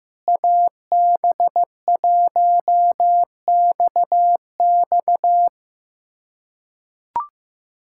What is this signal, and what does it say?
Morse code, heard as a keyed tone, about 15 wpm, 700 Hz. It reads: AB1XX
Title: K1USN SST QSO Practice - 15wpm